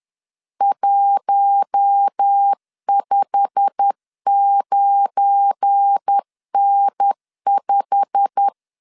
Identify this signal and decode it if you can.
Morse code, heard as a keyed tone, about 10 wpm, 800 Hz. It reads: 159N5
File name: LEZIONI / 8 YJ45 / 8 YJ45 10wpm 2549